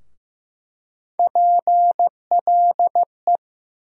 Morse code, heard as a keyed tone, about 15 wpm, 700 Hz. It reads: PLE
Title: USA Call Sign Postfixes - 3 Letters 15wpm